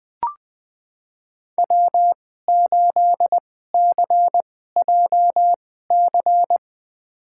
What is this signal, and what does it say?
Morse code, heard as a keyed tone, about 20 wpm, spaced as for 14 wpm, 700 Hz. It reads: W8CJC
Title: USA Call Signs - 5 Characters - 2x Character Spacing - 20wpm